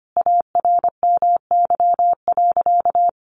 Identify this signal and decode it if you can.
Morse code, heard as a keyed tone, about 25 wpm, 700 Hz. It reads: ARMY.
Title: Sentences from Top 1400 Words 25wpm